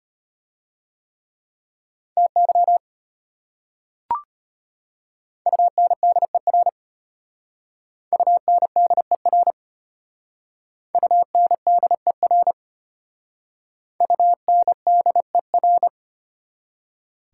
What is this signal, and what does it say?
Morse code, heard as a keyed tone, about 33 wpm, 700 Hz. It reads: TY UNDER UNDER UNDER UNDER